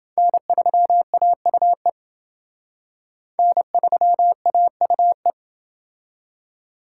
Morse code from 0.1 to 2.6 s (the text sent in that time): N3AUE